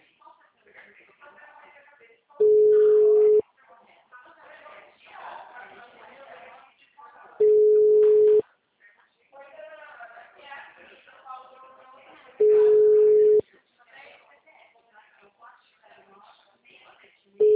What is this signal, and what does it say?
Ringback tone (Europe, 425 Hz), with speech.
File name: out- -2011- - - .18827